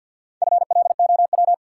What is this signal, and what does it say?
Morse code, heard as a keyed tone, about 52 wpm, 700 Hz. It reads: FLOW
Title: Sentences from Top 1200 Words - New Word Drill Alt - 50wpm